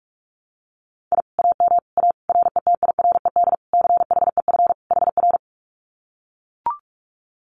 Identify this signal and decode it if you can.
Morse code, heard as a keyed tone, about 44 wpm, 700 Hz. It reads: I AM A RETIRED CHEF HR
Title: Common QSO Phrases Encore 45wpm